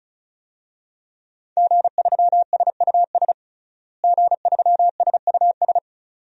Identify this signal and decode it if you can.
Morse code, heard as a keyed tone, about 36 wpm, 700 Hz. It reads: G3SUS G3SUS
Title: DXCC-100 Call Signs - 5 Characters - Repeated 2x - 35wpm